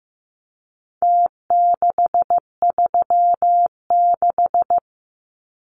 Morse code, heard as a keyed tone, about 15 wpm, 700 Hz. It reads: T636
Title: Numbers - Contest Style - 4-Digit 15wpm